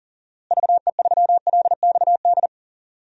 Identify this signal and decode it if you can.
Morse code, heard as a keyed tone, about 40 wpm, 700 Hz. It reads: VE3LXD
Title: DXCC-100 Call Signs - 6 Characters - Repeated 3x - 40wpm